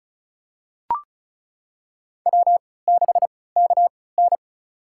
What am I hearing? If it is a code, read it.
Morse code, heard as a keyed tone, about 34 wpm, spaced as for 20 wpm, 700 Hz. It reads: W6KN
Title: USA Call Signs - 4 Characters - 3x Character Spacing - 35wpm